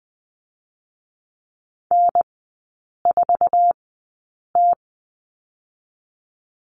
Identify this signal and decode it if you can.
Morse code, heard as a keyed tone, about 20 wpm, 700 Hz. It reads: N 4 T
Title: 145 Lesson 24e - Letter y - Three Characters 20wpm